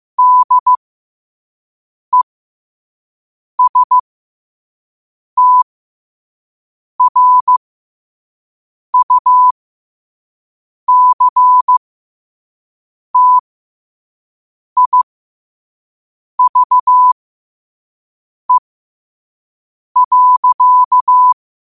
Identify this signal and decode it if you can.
Morse code, heard as a keyed tone, about 15 wpm, spaced as for 5 wpm, 1000 Hz. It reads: DESTRUCTIVE.
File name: qotdfiles / 5wpm